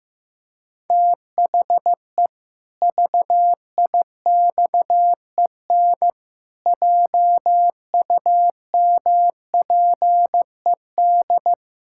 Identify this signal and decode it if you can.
Morse code, heard as a keyed tone, about 15 wpm, 700 Hz. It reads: THE VIXEN JUMPED
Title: Pangram Challenge v2 - 15wpm